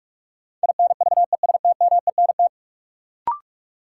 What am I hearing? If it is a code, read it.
Morse code, heard as a keyed tone, about 42 wpm, 700 Hz. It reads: INVESTMENT
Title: Sentences from Top 700 Words - New Word Drill - 45wpm